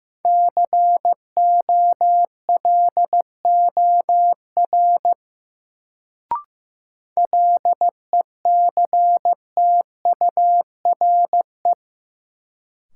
Morse code, heard as a keyed tone, about 15 wpm, 700 Hz. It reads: COLOR LECTURE
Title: Sentences from Top 1600 Words - New Word Drill Alt - 15wpm